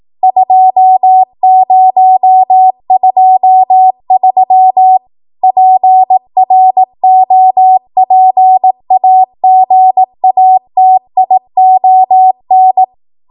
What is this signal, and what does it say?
Morse code, heard as a keyed tone, about 18 wpm, 750 Hz. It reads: 2023 PROPAGATION
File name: Archive / 18 WPM / 18WPM